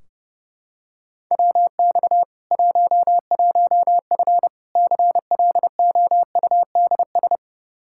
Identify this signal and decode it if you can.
Morse code, heard as a keyed tone, about 30 wpm, 700 Hz. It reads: WX 11F CLOUDS
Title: Practice QSO - Exchange 2 - Info - 30wpm